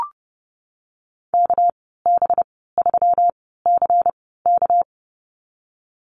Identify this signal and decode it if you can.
Morse code, heard as a keyed tone, about 30 wpm, spaced as for 17 wpm, 700 Hz. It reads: KB3CK
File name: Lesson 37g - Letter x - Call Signs 30wpm